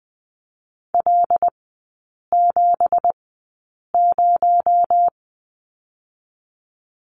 Morse code, heard as a keyed tone, about 20 wpm, spaced as for 8 wpm, 700 Hz. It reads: L70
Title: 218 Lesson 35e - Number 0 - Three Characters 20wpm